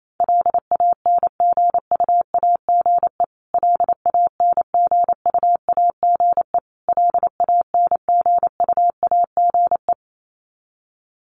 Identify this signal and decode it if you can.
Morse code, heard as a keyed tone, about 28 wpm, 700 Hz. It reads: LANGUAGE LANGUAGE LANGUAGE
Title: Top 500 Words Repeated 28wpm